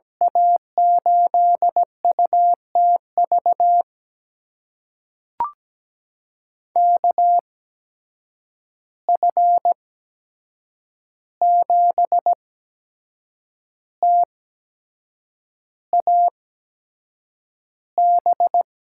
Morse code, heard as a keyed tone, about 17 wpm, spaced as for 7 wpm, 700 Hz. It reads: A8UTV KF7TAB
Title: USA Call Signs - 6 Characters - 8x Character Spacing - 17wpm